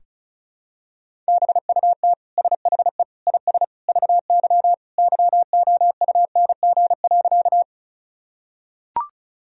Morse code, heard as a keyed tone, about 36 wpm, 700 Hz. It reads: BUT SHE IS VY YOUNG.
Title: Sentences from Top 400 Words and Abbreviations 35wpm